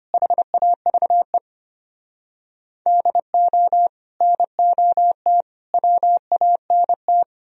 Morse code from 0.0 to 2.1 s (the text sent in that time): HAVE